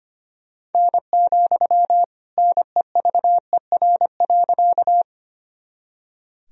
Morse code, heard as a keyed tone, about 25 wpm, 700 Hz. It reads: N, NEVER.
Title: Sentences from Top 300 Words and Abbreviations 25wpm